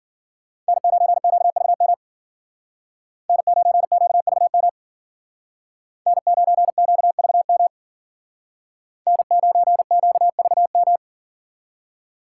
Morse code, heard as a keyed tone, about 52 wpm, 700 Hz. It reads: N9QVM N9QVM N9QVM N9QVM